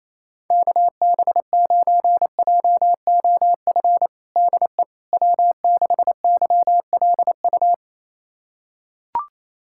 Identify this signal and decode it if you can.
Morse code, heard as a keyed tone, about 28 wpm, 700 Hz. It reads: KB9JOF DE W6YLU
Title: USA Callsign DE Callsign 28wpm